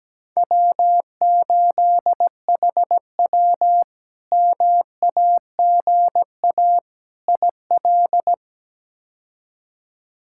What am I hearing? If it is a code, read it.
Morse code, heard as a keyed tone, about 17 wpm, 700 Hz. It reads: W8HW MAGA IL
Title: K1USN SST QSO Practice - 17wpm